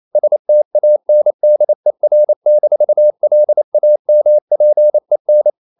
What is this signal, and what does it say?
Morse code, heard as a keyed tone, about 28 wpm, 600 Hz. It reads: STANDER-LAMPEN
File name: Ferie QRQ 28wpm